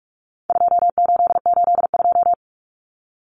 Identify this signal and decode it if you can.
Morse code, heard as a keyed tone, about 44 wpm, 700 Hz. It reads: 2882
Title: Common Cut Numbers - 4-Digit 45wpm